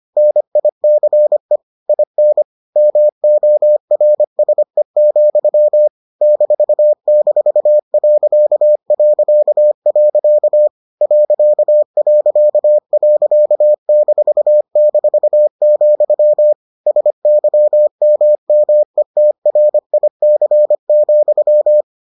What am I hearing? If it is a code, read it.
Morse code, heard as a keyed tone, about 25 wpm, 600 Hz. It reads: NICE IN MORSE, --... ...--, SYMMETRIC,